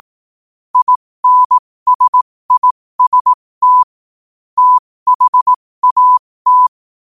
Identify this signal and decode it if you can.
Morse code, heard as a keyed tone, about 18 wpm, spaced as for 15 wpm, 1000 Hz. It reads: INSIST THAT
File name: qotdfiles / 15wpm